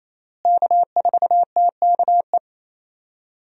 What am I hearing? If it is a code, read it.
Morse code, heard as a keyed tone, about 28 wpm, 700 Hz. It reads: K4TKE
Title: USA Call Signs - 5 Characters - Repeated 3x - 28wpm